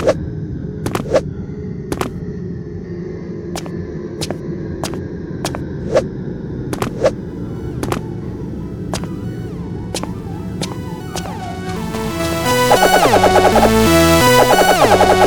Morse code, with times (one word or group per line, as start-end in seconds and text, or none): none